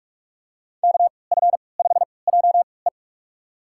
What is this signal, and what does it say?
Morse code, heard as a keyed tone, about 46 wpm, spaced as for 26 wpm, 700 Hz. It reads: KF5JE
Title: USA Call Signs - 5 Characters - 3x Character Spacing - 45wpm